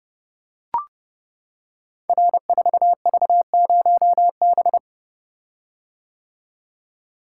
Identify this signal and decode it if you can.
Morse code, heard as a keyed tone, about 30 wpm, 700 Hz. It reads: R4V0B
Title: Code Groups - 5 Letter:Numbers - 30wpm